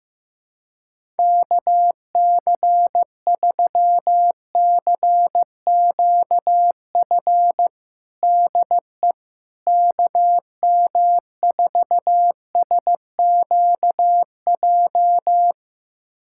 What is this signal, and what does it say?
Morse code, heard as a keyed tone, about 15 wpm, 700 Hz. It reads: KC3CQF DE KM4SQJ
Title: USA Callsign DE Callsign - 6 Characters - 15wpm